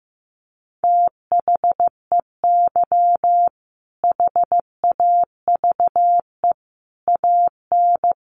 Morse code, heard as a keyed tone, about 15 wpm, 700 Hz. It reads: THEY HAVE AN
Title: Sentences from Top 1500 Words 15wpm